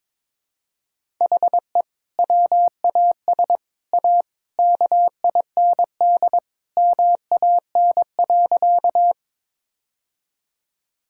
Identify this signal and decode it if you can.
Morse code, heard as a keyed tone, about 21 wpm, 700 Hz. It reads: HE WAS A KIND MAN.
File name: practice / 22wpm / Sentences with 3-Word Phrases - 22wpm